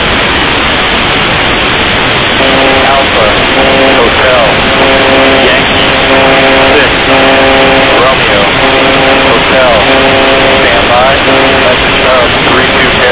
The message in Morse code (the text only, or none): none